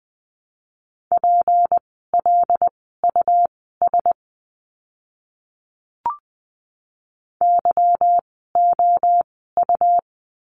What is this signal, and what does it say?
Morse code, heard as a keyed tone, about 20 wpm, spaced as for 14 wpm, 700 Hz. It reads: PLUS YOU
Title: 153 Lesson 25f - Letter p - Words 20wpm